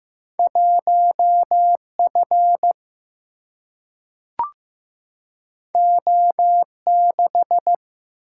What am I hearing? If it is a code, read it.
Morse code, heard as a keyed tone, about 15 wpm, 700 Hz. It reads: 1F O6
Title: Single-Letter Single-Number 15wpm